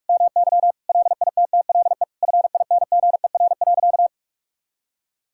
Morse code, heard as a keyed tone, about 46 wpm, 700 Hz. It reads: MY LITTLE FINGER.
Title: Sentences from Top 1600 Words 45wpm